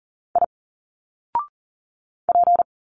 Morse code, heard as a keyed tone, about 40 wpm, 700 Hz. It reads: I P
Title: Single Letters 40wpm